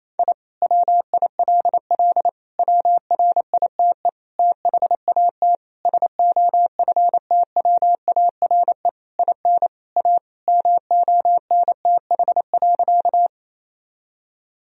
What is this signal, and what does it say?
Morse code, heard as a keyed tone, about 28 wpm, 700 Hz. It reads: I WILL WRITE THAT SOFTWARE IN A MONTH.
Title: Sentences from Top 1000 Words - Review Part 3 - 28wpm